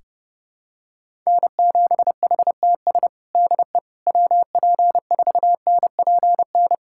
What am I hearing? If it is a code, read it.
Morse code, heard as a keyed tone, about 30 wpm, 700 Hz. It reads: N7HTS DE WP4NPN
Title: USA Callsign DE Callsign 30wpm